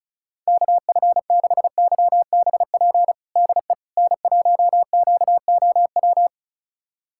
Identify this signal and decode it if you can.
Morse code, heard as a keyed tone, about 34 wpm, 700 Hz. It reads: KF6YBP DE N1QOW
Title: USA Callsign DE Callsign 35wpm